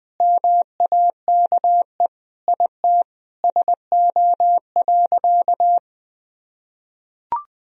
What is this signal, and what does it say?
Morse code, heard as a keyed tone, about 20 wpm, 700 Hz. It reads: MAKE IT SO.
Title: Sentences from Top 1000 Words - Review Part 3 - 20wpm